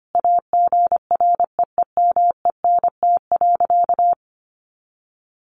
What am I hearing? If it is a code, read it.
Morse code, heard as a keyed tone, about 25 wpm, 700 Hz. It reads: AGREEMENT.
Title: Sentences from Top 1500 Words 25wpm